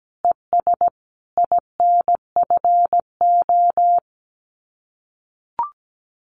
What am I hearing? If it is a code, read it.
Morse code, heard as a keyed tone, about 17 wpm, 700 Hz. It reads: ES INFO
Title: Common QSO Phrases Encore 17wpm